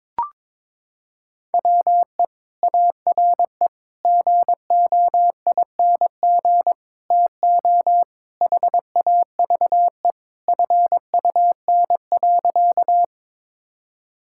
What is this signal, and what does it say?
Morse code, heard as a keyed tone, about 21 wpm, 700 Hz. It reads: WE ARE GOING TO HAVE FUN.
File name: practice / 22wpm / Sentences with 4-Word Phrases - 22wpm